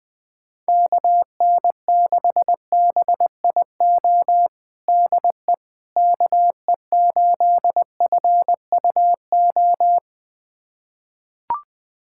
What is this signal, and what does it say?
Morse code, heard as a keyed tone, about 20 wpm, 700 Hz. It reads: KN6BIO DE KE8FUO
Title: USA Callsign DE Callsign - 6 Characters - 20wpm